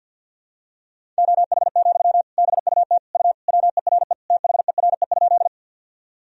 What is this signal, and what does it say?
Morse code, heard as a keyed tone, about 48 wpm, 700 Hz. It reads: OH, BUT U WERE THERE?
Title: Sentences from Top 500 Words and Abbreviations 50wpm